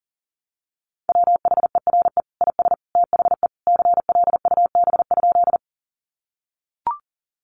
Morse code, heard as a keyed tone, about 40 wpm, 700 Hz. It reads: WHERE IS THE CLUB?
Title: Sentences from Top 1200 Words 40wpm